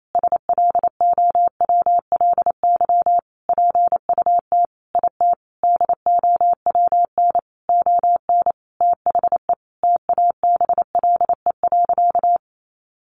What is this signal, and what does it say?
Morse code, heard as a keyed tone, about 28 wpm, 700 Hz. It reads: SLOWLY PUT IT DOWN ON THE TABLE.
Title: Sentences from Top 1100 Words 28wpm